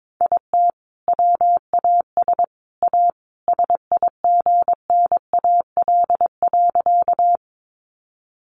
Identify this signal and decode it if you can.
Morse code, heard as a keyed tone, about 22 wpm, 700 Hz. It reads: IT WAS A SIGNAL.
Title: Sentences from Top 1400 Words 22wpm